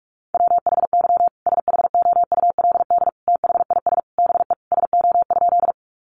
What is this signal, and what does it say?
Morse code, heard as a keyed tone, about 44 wpm, 700 Hz. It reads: WHY SHOULD THIS BE SO?
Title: Sentences from Top 300 Words 45wpm